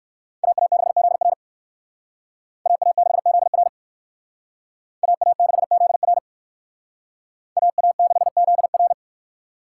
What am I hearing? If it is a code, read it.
Morse code, heard as a keyed tone, about 58 wpm, 700 Hz. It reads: AA6ZR AA6ZR AA6ZR AA6ZR